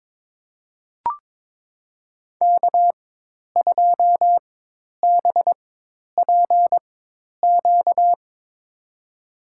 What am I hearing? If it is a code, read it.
Morse code, heard as a keyed tone, about 22 wpm, spaced as for 10 wpm, 700 Hz. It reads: K2BPQ
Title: USA Call Signs - 5 Characters - 4x Character Spacing - 22wpm